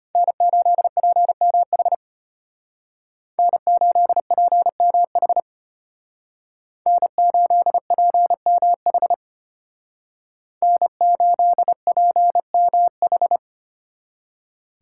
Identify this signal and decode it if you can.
Morse code, heard as a keyed tone, about 32 wpm, 700 Hz. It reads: N8PMH N8PMH N8PMH N8PMH